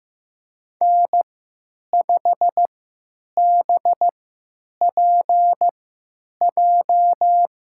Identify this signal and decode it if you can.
Morse code, heard as a keyed tone, about 15 wpm, spaced as for 9 wpm, 700 Hz. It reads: N5BPJ
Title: USA Call Signs - 5 Characters - 3x Character Spacing - 15wpm